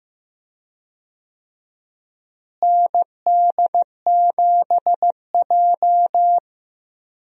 Morse code, heard as a keyed tone, about 15 wpm, 700 Hz. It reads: ND7J
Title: USA Call Signs - 4 Characters - Repeated 3x - 15wpm